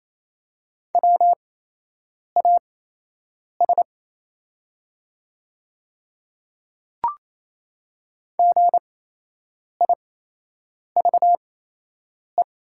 Morse code, heard as a keyed tone, about 27 wpm, spaced as for 8 wpm, 690 Hz. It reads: WAS GIVE
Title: 8x Character Spacing - Top 200 Words - 28wpm